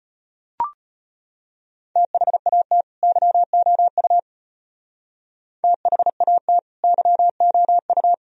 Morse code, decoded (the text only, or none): THAT YOU THAT YOU